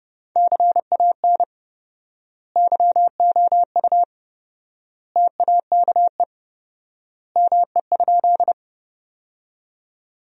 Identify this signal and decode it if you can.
Morse code, heard as a keyed tone, about 30 wpm, 700 Hz. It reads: CAN YOU TAKE ME?
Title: 4x Word-Spacing - Sentences from Top 200 Words - 30wpm